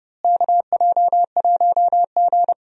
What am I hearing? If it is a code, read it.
Morse code, heard as a keyed tone, about 29 wpm, 700 Hz. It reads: KJ1G
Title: USA Call Signs - 4 Characters - Repeated 3x - 30wpm